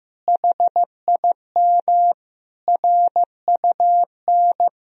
Morse code, heard as a keyed tone, about 15 wpm, 700 Hz. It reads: HIM RUN
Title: Sentences from Top 1000 Words - Review Part 1 - 15wpm